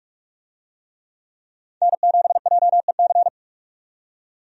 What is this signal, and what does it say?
Morse code, heard as a keyed tone, about 45 wpm, 700 Hz. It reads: N7JEC
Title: USA Call Signs - 5 Characters - 45wpm